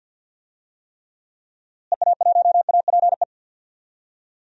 Morse code, heard as a keyed tone, about 52 wpm, 700 Hz. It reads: EA1APE
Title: DXCC-100 Call Signs - 6 Characters - Repeated 2x - 50wpm